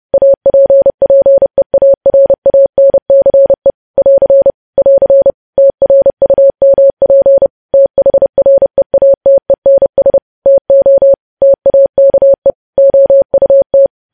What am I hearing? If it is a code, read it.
Morse code, heard as a keyed tone, about 30 wpm, 550 Hz. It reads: APPEARANCE <AR> <AR> TRUMP THREATENS TO TAKE OUT